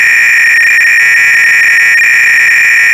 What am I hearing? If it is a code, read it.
300-baud modem data.